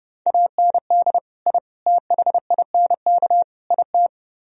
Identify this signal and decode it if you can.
Morse code, heard as a keyed tone, about 30 wpm, 700 Hz. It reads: AND I THINK IT